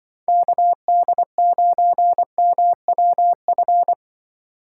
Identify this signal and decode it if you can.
Morse code, heard as a keyed tone, about 24 wpm, 700 Hz. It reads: KD9MWF
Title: Speed Racing - USA 2x3 Call Signs 20wpm